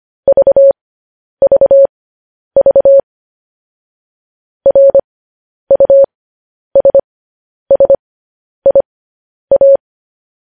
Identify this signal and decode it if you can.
Morse code, heard as a keyed tone, about 25 wpm, spaced as for 10 wpm, 550 Hz. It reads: VVV RUSSIA
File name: podcast / News headlines-1x-25x10